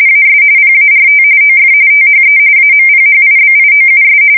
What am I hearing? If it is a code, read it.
RTTY, about 45.45 baud. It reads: AS PART OF THE AMPHIBIOUS
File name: fsk170 / USS-Missouri